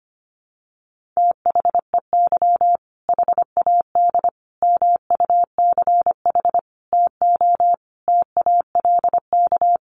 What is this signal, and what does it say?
Morse code, heard as a keyed tone, about 25 wpm, 700 Hz. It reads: THEY HAD MUCH TO TALK
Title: Sentences from Top 400 Words 25wpm